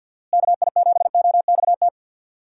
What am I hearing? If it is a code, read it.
Morse code, heard as a keyed tone, about 47 wpm, 700 Hz. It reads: KI7OXT
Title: USA Call Signs - 6 Characters - Repeated 2x - 50wpm